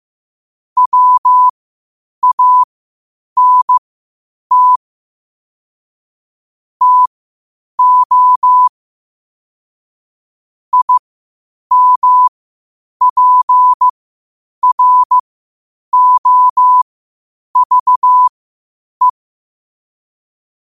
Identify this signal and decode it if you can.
Morse code, heard as a keyed tone, about 15 wpm, spaced as for 8 wpm, 1000 Hz. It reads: WANT TO IMPROVE
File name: qotdfiles / 8wpm